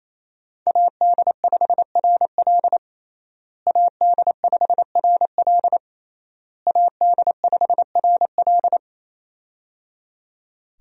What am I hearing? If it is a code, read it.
Morse code, heard as a keyed tone, about 28 wpm, 700 Hz. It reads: AD5RL AD5RL AD5RL
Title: USA Call Signs - 5 Characters - Repeated 3x - 28wpm